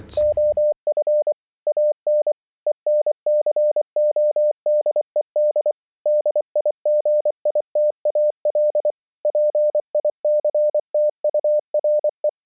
Morse code, heard as a keyed tone, about 25 wpm, 600 Hz. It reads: OF AN ENCODED DIGITAL PICTURE